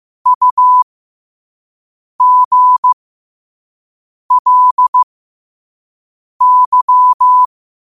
Morse code, heard as a keyed tone, about 15 wpm, spaced as for 5 wpm, 1000 Hz. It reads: UGLY
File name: qotdfiles / 5wpm